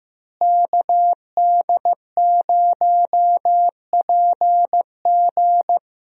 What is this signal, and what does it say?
Morse code, heard as a keyed tone, about 15 wpm, 700 Hz. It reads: KD0PG
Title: USA Call Signs - 5 Characters - Repeated 3x - 15wpm